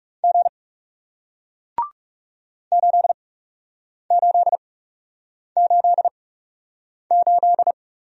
Morse code, heard as a keyed tone, about 40 wpm, spaced as for 8 wpm, 700 Hz. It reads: G 8888